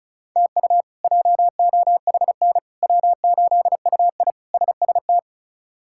Morse code, heard as a keyed tone, about 34 wpm, 700 Hz. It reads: TU JOHN W8UI SST
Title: K1USN SST QSO Practice - 35wpm